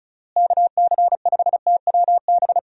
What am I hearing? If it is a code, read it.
Morse code, heard as a keyed tone, about 36 wpm, 700 Hz. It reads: KC5TWB
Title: USA Call Signs - 6 Characters - Repeated 2x - 35wpm